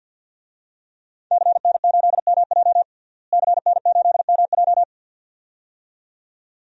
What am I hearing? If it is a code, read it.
Morse code, heard as a keyed tone, about 52 wpm, 700 Hz. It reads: CN8MJ CN8MJ
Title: DXCC-100 Call Signs - 5 Characters - Repeated 2x - 50wpm